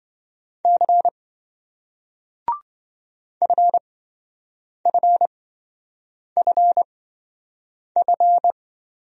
Morse code, heard as a keyed tone, about 27 wpm, spaced as for 7 wpm, 700 Hz. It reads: C FFFF